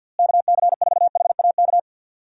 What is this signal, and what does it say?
Morse code, heard as a keyed tone, about 50 wpm, 700 Hz. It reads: KC4HAK
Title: USA Call Signs - 6 Characters - Repeated 2x - 50wpm